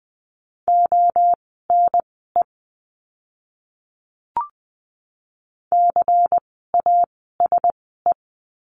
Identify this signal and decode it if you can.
Morse code, heard as a keyed tone, about 20 wpm, spaced as for 14 wpm, 700 Hz. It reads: ONE CASE
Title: 094 Lesson 16f - Letter c - Words 20wpm